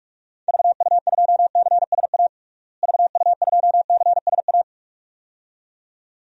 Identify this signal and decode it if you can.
Morse code, heard as a keyed tone, about 46 wpm, 700 Hz. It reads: VU2CSA VU2CSA
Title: DXCC-100 Call Signs - 6 Characters - Repeated 2x - 45wpm